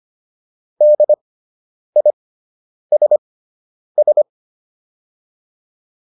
Morse code, heard as a keyed tone, about 25 wpm, spaced as for 9 wpm, 600 Hz. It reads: DISS
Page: fundamental-practice-files